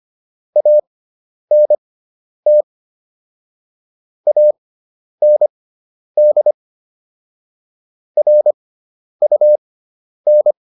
Morse code, heard as a keyed tone, about 25 wpm, spaced as for 10 wpm, 600 Hz. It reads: ANT AND RUN